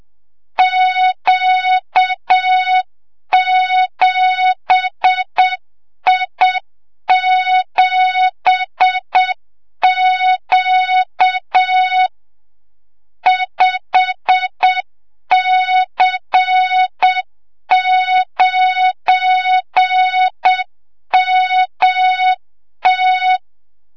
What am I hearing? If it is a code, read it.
Morse code, heard as a keyed tone, about 7 wpm, 3700 Hz, with noise. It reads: Q7I7Q 5C9MT
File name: Morze-HA2MN / hang / 25 GYAKORLAT